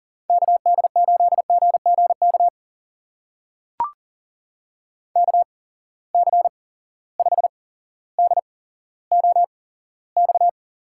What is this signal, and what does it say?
Morse code, heard as a keyed tone, about 40 wpm, spaced as for 17 wpm, 700 Hz. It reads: KD8GGK KC5DOX
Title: USA Call Signs - 6 Characters - 8x Character Spacing - 40wpm